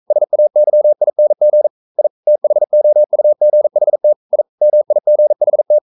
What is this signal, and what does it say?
Morse code, heard as a keyed tone, about 42 wpm, 600 Hz. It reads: SAYING I THOUGHT I MIGHT
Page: Romanen Never af Ken Follet på engelsk.